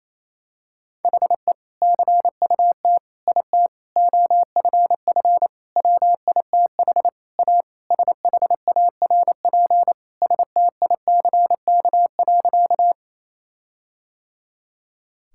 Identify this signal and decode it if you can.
Morse code, heard as a keyed tone, about 28 wpm, 700 Hz. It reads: HE CUT IT OFF WITH A SHARP STICK.
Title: Sentences from Top 1600 Words 28wpm